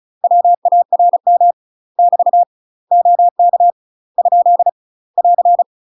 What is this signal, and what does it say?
Morse code, heard as a keyed tone, about 34 wpm, 700 Hz. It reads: WARM <BT> OK ? <AR>
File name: UK MASTER QSOs TRAINING / 35WPM / UK MASTER QSOs 0037